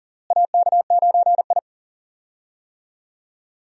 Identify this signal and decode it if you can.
Morse code, heard as a keyed tone, about 40 wpm, 700 Hz. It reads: AK9I
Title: USA Call Signs - 4 Characters - Repeated 3x - 40wpm